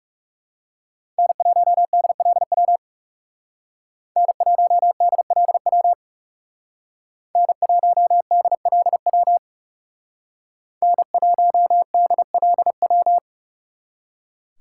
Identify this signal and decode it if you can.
Morse code, heard as a keyed tone, about 35 wpm, 700 Hz. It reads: N1DLW N1DLW N1DLW N1DLW
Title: Speed Racing - USA 1x3 Call Signs 30wpm